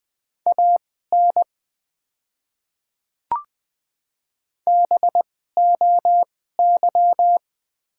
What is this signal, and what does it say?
Morse code, heard as a keyed tone, about 20 wpm, spaced as for 15 wpm, 690 Hz. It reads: AN BOY
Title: 2x Character Spacing - Top 300 Words - 20wpm